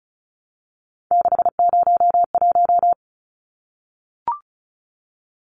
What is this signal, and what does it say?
Morse code, heard as a keyed tone, about 34 wpm, 700 Hz. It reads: 601
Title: Numbers - 3-Digit 35wpm